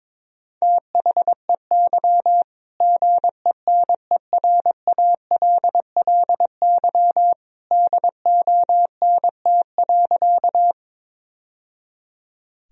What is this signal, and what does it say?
Morse code, heard as a keyed tone, about 22 wpm, 700 Hz. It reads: THEY GENERALLY DONT.
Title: Sentences from Top 1000 Words - Review Part 4 - 22wpm